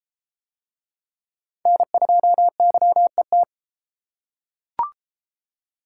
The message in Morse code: N2YET

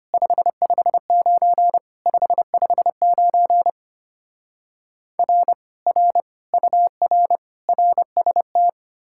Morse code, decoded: 559 559 R R UR RST

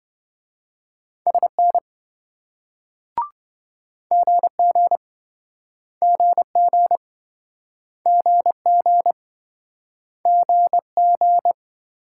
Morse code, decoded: SN GG GG GG GG

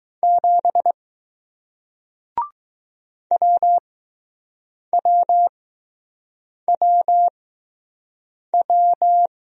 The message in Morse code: 7 WWWW

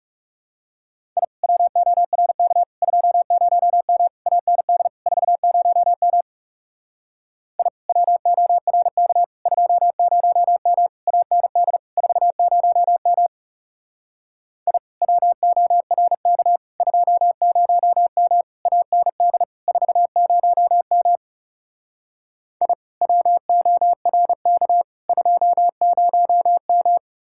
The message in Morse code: I WORK 20M AND 40M I WORK 20M AND 40M I WORK 20M AND 40M I WORK 20M

